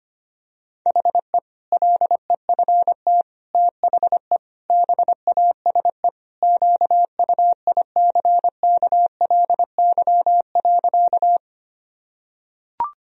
Morse code, heard as a keyed tone, about 25 wpm, 700 Hz. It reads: HE LEFT THE BASE QUICKLY.
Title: Sentences from Top 1000 Words - Review Part 1 - 25wpm